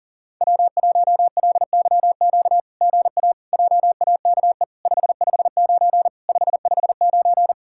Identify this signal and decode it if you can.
Morse code, heard as a keyed tone, about 40 wpm, 700 Hz. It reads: W1LYQ GA JAKE 559 559
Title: SOTA QSO Practice 40wpm